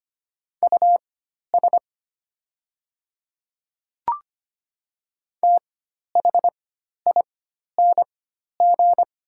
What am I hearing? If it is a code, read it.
Morse code, heard as a keyed tone, about 25 wpm, spaced as for 12 wpm, 690 Hz. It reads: US THING
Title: 4x Character Spacing - Top 500 Words - 25wpm